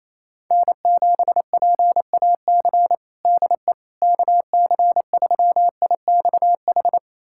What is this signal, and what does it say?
Morse code, heard as a keyed tone, about 28 wpm, 700 Hz. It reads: N7PAC DE KC3IXH
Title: USA Callsign DE Callsign 28wpm